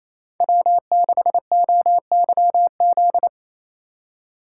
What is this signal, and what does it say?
Morse code, heard as a keyed tone, about 32 wpm, 700 Hz. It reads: W6OYZ
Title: Speed Racing - USA 1x3 Call Signs 28wpm